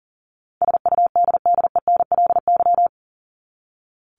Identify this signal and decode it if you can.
Morse code, heard as a keyed tone, about 40 wpm, 700 Hz. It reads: SUDDENLY
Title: LY Suffix - Top 100 Words with Base Words 40wpm